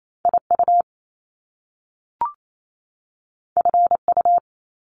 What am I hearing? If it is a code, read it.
Morse code, heard as a keyed tone, about 28 wpm, 700 Hz. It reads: IU FU
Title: Top Two Letter Combinations in Words 28wpm